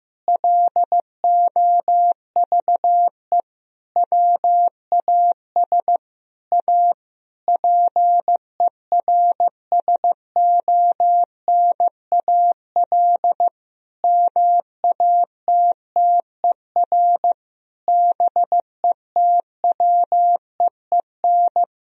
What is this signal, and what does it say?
Morse code, heard as a keyed tone, about 15 wpm, 700 Hz. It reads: LOVE WAS A PERSONAL MATTER BETWEEN
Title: Sentences from Top 1000 Words - Review Part 4 - 15wpm